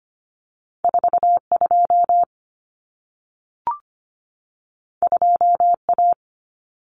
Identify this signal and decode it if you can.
Morse code, heard as a keyed tone, about 25 wpm, 700 Hz. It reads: 42 2A
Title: Common Cut Numbers - 2-Digit 25wpm